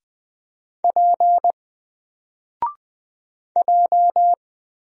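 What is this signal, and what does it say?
Morse code, heard as a keyed tone, about 20 wpm, 700 Hz. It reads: P J
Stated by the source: Single letter and number Morse drill at 20 wpm, from N4LCD.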